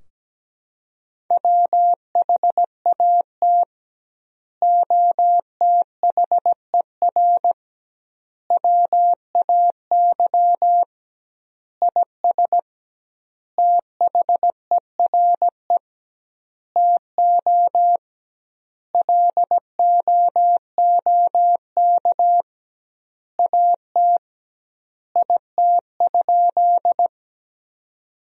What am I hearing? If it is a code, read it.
Morse code, heard as a keyed tone, about 17 wpm, 700 Hz. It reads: WHAT OTHER WAY IS THERE TO LOOK AT IT?
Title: 2x Word-Spacing - Sentences from Top 100 Words - 17wpm